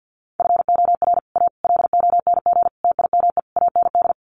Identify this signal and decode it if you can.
Morse code, heard as a keyed tone, about 50 wpm, 700 Hz. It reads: FOR A LONG TIME AND
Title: Sets of 5 Words Encore2 50wpm